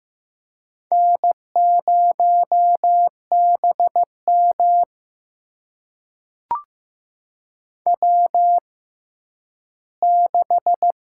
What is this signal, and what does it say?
Morse code, heard as a keyed tone, about 15 wpm, spaced as for 10 wpm, 700 Hz. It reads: N0BM W6
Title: USA Call Signs - 4 Characters - 6x Character Spacing - 15wpm